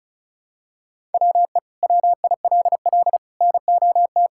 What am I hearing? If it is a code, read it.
Morse code, heard as a keyed tone, about 34 wpm, 700 Hz. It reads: WE WILL NOT
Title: Sentences from Top 1000 Words - Review Part 5 - 35wpm